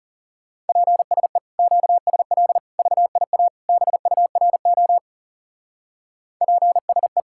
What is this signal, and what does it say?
Morse code, heard as a keyed tone, about 34 wpm, 700 Hz. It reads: PSE QSL VIA BURO PSE
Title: Speed-Racing Common QSO Phrases Encore - 30wpm